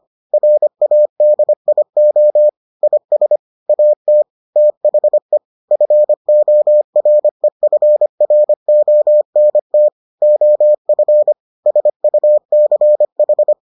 Morse code, heard as a keyed tone, about 25 wpm, 600 Hz. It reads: RADIO IS AT THE FOREFRONT OF SUCH